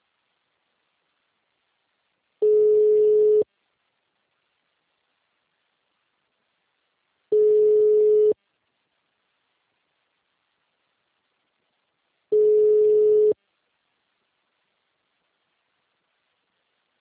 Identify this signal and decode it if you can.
Ringback tone (Europe, 425 Hz).
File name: out- -2010- - - .68172